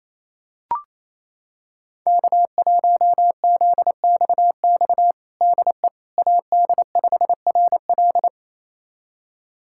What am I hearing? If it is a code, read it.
Morse code, heard as a keyed tone, about 28 wpm, 700 Hz. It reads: K1ZXX DE AD5RL
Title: USA Callsign DE Callsign - 5 Characters - 28wpm